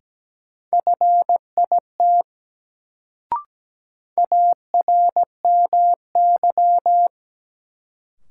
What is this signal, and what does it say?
Morse code, heard as a keyed tone, about 17 wpm, 700 Hz. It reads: FIT ARMY
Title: Sentences from Top 800 Words - New Word Drill - 17wpm